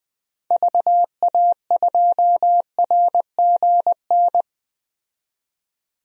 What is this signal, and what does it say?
Morse code, heard as a keyed tone, about 20 wpm, 700 Hz. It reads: VA2RGN
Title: DXCC-100 Call Signs - 6 Characters - 20wpm